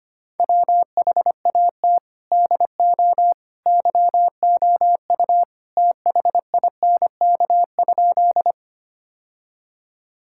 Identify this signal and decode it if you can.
Morse code, heard as a keyed tone, about 25 wpm, 700 Hz. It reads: WHAT DO YOU THINK?
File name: practice / 25wpm / Sentences with 2-Word Phrases - 25wpm